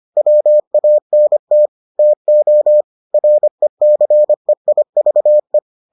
Morse code, heard as a keyed tone, about 25 wpm, 600 Hz. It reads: WANT TO RECEIVE